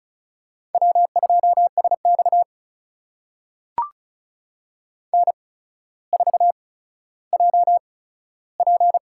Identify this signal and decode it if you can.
Morse code, heard as a keyed tone, about 36 wpm, spaced as for 15 wpm, 700 Hz. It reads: W2SX N4JP
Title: USA Call Signs - 4 Characters - 8x Character Spacing - 35wpm